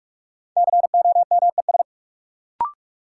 Morse code, heard as a keyed tone, about 41 wpm, 700 Hz. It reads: COMES
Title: Sets of 3 Words - New Word Drill - 45wpm